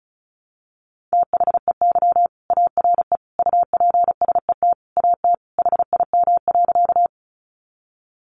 Morse code, heard as a keyed tone, about 34 wpm, 700 Hz. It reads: THEY ARE UPSET AT HIM.
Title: Sentences from Top 800 Words 35wpm